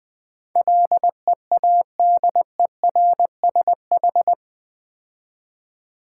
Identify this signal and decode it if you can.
Morse code, heard as a keyed tone, about 20 wpm, 700 Hz. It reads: LEADERSH
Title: Build A Word - 4 Letters At A Time - 10-Letter Words - 20wpm